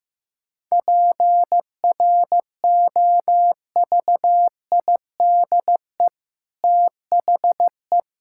Morse code, heard as a keyed tone, about 15 wpm, 700 Hz. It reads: PROVIDE THE
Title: Sentences from Top 1000 Words - Review Part 5 - 15wpm